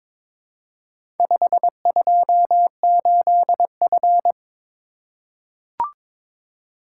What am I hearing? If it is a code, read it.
Morse code, heard as a keyed tone, about 22 wpm, 700 Hz. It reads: 528F
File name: practice / 22wpm / Four Letter-Number Combinations - 22wpm